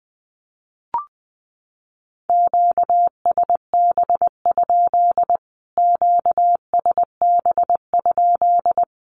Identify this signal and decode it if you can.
Morse code, heard as a keyed tone, about 20 wpm, 700 Hz. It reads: QSB? QSB?
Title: Q-Codes Repeated 3x 20wpm